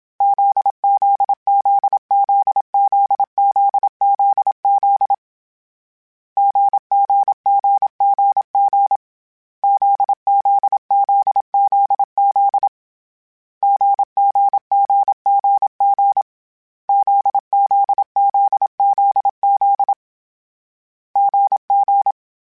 Morse code, heard as a keyed tone, about 27 wpm, 800 Hz. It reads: ZZZZZZZZ GGGGG ZZZZZ GGGGG ZZZZZ GG